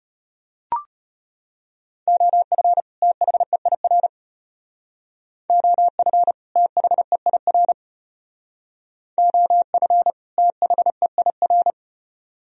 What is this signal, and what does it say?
Morse code, heard as a keyed tone, about 34 wpm, 700 Hz. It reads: OF THEIR OF THEIR OF THEIR